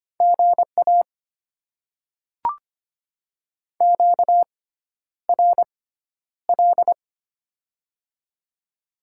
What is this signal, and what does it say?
Morse code, heard as a keyed tone, about 25 wpm, spaced as for 11 wpm, 690 Hz. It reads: GA QRL
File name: Top 20 QSO Elements - 6x Character Spacing - 25wpm